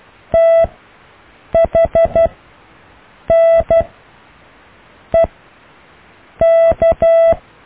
Morse code, heard as a keyed tone, about 12 wpm, spaced as for 6 wpm, 660 Hz. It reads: THNEK
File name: cw / EKTPV+BFHNW